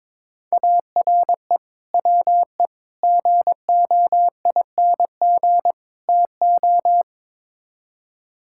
Morse code, heard as a keyed tone, about 22 wpm, 700 Hz. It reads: ARE WE GOING TO
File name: practice / 22wpm / Build A Phrase - 5-Word Phrases - 22wpm